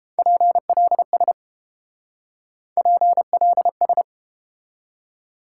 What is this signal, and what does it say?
Morse code, heard as a keyed tone, about 31 wpm, 700 Hz. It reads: PLS PLS